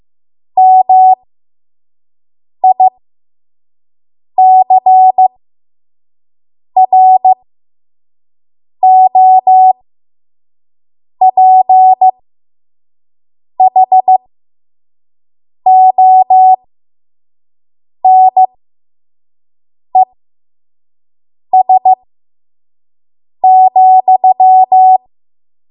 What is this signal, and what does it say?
Morse code, heard as a keyed tone, about 15 wpm, spaced as for 5 wpm, 750 Hz. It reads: MICROPHONES,
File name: Morse / 05WPM